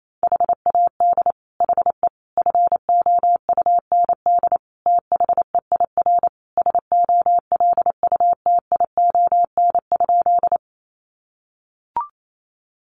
Morse code, heard as a keyed tone, about 28 wpm, 700 Hz. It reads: HAD HE FOUND THEIR SOLUTION?
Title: Sentences from Top 1300 Words 28wpm